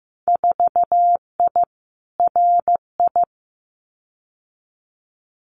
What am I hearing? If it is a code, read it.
Morse code, heard as a keyed tone, about 15 wpm, 700 Hz. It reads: 4I RI
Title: Winter Field Day Exchanges Encore 15wpm